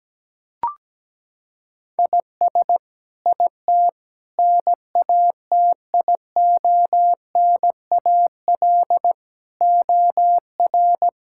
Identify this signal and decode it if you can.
Morse code, heard as a keyed tone, about 17 wpm, 700 Hz. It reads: IS IT NATIONAL OR
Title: Sentences from Top 1000 Words - Review Part 5 - 17wpm